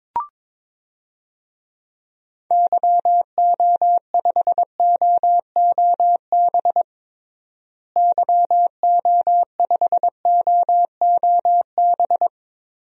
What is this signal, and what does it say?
Morse code, heard as a keyed tone, about 22 wpm, 700 Hz. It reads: YO5OOB YO5OOB
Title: DXCC-100 Call Signs - 6 Characters - Repeated 3x - 22wpm